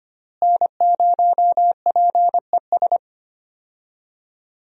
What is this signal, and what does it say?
Morse code, heard as a keyed tone, about 25 wpm, 700 Hz. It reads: N0PES
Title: USA Call Signs - 5 Characters - Repeated 3x - 25wpm